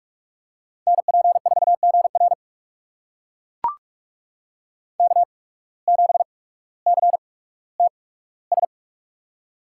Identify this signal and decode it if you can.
Morse code, heard as a keyed tone, about 42 wpm, spaced as for 19 wpm, 700 Hz. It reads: NP4GR K7CTS
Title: USA Call Signs - 5 Characters - 8x Character Spacing - 45wpm